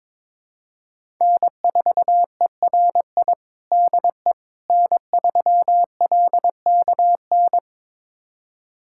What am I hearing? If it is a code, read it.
Morse code, heard as a keyed tone, about 22 wpm, 700 Hz. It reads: N4ERI DE N3LKN
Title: USA Callsign DE Callsign - 5 Characters - 22wpm